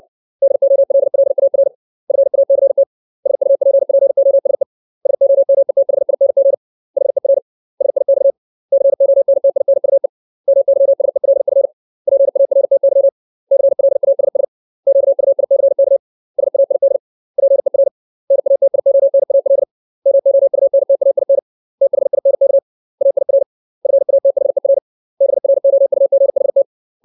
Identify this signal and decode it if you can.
Morse code, heard as a keyed tone, about 60 wpm, 550 Hz. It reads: DOLLAR FACT SUPPOSE SOMETHING HER SIX CONTINUE MYSELF PARTY CLASS QUICK SAID PER NATIONAL MOUNTAIN THINK AIR FATHER BROUGHT
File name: 632 words / 632 Words 60WPM